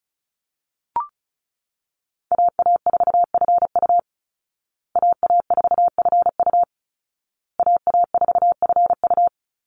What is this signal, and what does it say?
Morse code, heard as a keyed tone, about 34 wpm, 700 Hz. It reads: AA4FU AA4FU AA4FU
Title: Call Signs Repeated 2x - 35wpm